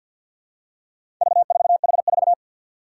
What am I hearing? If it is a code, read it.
Morse code, heard as a keyed tone, about 49 wpm, 700 Hz. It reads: V4H4
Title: Four Challenging Letter-Number Combinations 50wpm